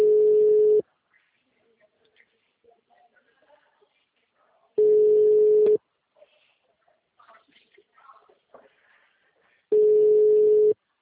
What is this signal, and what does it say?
Ringback tone (Europe, 425 Hz), with speech.